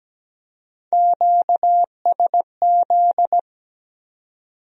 Morse code, heard as a keyed tone, about 20 wpm, 700 Hz. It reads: QSZ
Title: Speed-Racing Q-Codes - 17wpm